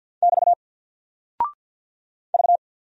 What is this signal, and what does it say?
Morse code, heard as a keyed tone, about 50 wpm, 700 Hz. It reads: <BT> V
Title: Single Character (Letter:Number:Punctuation:Prosign) 50wpm